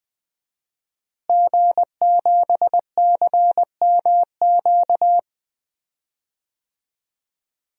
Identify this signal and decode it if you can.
Morse code, heard as a keyed tone, about 20 wpm, 700 Hz. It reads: G7CMQ
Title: DXCC-100 Call Signs - 5 Characters - Repeated 2x - 20wpm